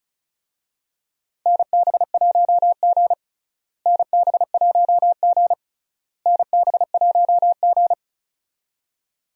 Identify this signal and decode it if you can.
Morse code, heard as a keyed tone, about 36 wpm, 700 Hz. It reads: NB1G NB1G NB1G
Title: USA Call Signs - 4 Characters - Repeated 3x - 35wpm